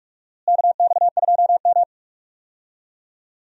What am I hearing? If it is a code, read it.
Morse code, heard as a keyed tone, about 43 wpm, 700 Hz. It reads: KX2M
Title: USA Call Signs - 4 Characters - Repeated 2x - 45wpm